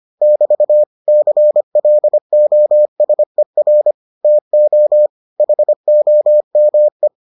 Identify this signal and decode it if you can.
Morse code, heard as a keyed tone, about 25 wpm, 600 Hz. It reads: <BT> CLOSER TO HOME